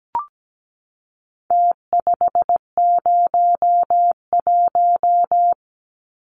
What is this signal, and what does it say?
Morse code, heard as a keyed tone, about 17 wpm, 700 Hz. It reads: T501
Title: Numbers - Contest Style - 4-Digit 17wpm